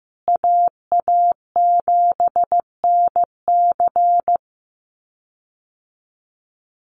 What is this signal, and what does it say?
Morse code, heard as a keyed tone, about 14 wpm, 700 Hz. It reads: AA7NC
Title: USA Call Signs Repeated 3x 15wpm